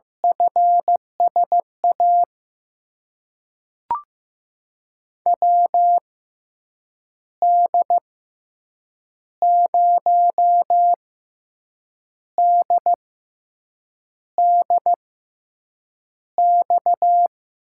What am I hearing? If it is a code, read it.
Morse code, heard as a keyed tone, about 15 wpm, spaced as for 6 wpm, 700 Hz. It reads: FSA WD0DDX
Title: USA Call Signs - 6 Characters - 6x Character Spacing - 15wpm